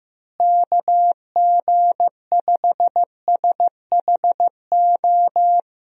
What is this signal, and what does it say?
Morse code, heard as a keyed tone, about 15 wpm, 700 Hz. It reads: KG5SHO
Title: USA Call Signs - 6 Characters - Repeated 2x - 15wpm